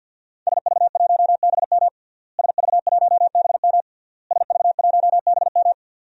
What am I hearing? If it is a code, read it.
Morse code, heard as a keyed tone, about 53 wpm, 700 Hz. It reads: SV1BM SV1BM SV1BM
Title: DXCC-100 Call Signs - 5 Characters - Repeated 3x - 50wpm